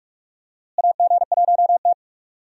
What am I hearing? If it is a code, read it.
Morse code, heard as a keyed tone, about 44 wpm, 700 Hz. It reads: AG1T
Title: USA Call Signs - 4 Characters - Repeated 3x - 45wpm